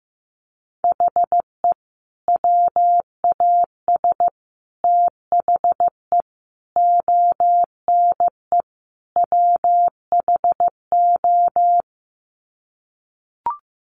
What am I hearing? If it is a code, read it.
Morse code, heard as a keyed tone, about 15 wpm, 700 Hz. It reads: HE WAS THE ONE WHO
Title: Sets of 5 Words Encore2 15wpm